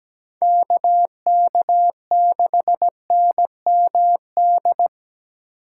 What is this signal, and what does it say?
Morse code, heard as a keyed tone, about 20 wpm, 700 Hz. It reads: KK6NMD
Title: Speed Racing - USA Call Signs 17wpm